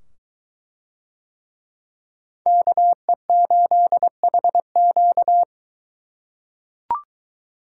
Morse code, heard as a keyed tone, about 22 wpm, 700 Hz. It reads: KE8HQ